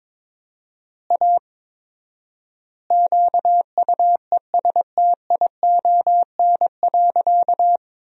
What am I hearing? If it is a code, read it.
Morse code, heard as a keyed tone, about 22 wpm, 700 Hz. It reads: A QUESTION.
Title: 4x Word-Spacing - Sentences from Top 400 Words - 22wpm